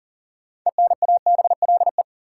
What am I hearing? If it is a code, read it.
Morse code, heard as a keyed tone, about 40 wpm, 700 Hz. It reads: ENABLE
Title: Sentences from Top 1500 Words - New Word Drill Alt - 40wpm